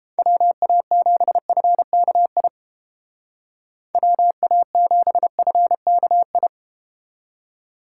Morse code, heard as a keyed tone, about 30 wpm, 700 Hz. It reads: WA7FKI WA7FKI